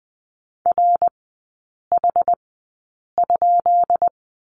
Morse code, heard as a keyed tone, about 20 wpm, spaced as for 8 wpm, 700 Hz. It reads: RH?
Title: 152 Lesson 25e - Letter p - Three Characters 20wpm